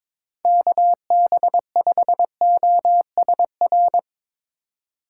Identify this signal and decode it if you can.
Morse code, heard as a keyed tone, about 24 wpm, 700 Hz. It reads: KB5OSR
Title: Speed Racing - USA 2x3 Call Signs 22wpm